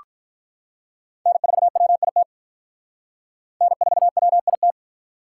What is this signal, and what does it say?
Morse code, heard as a keyed tone, about 46 wpm, 700 Hz. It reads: N4WIT N4WIT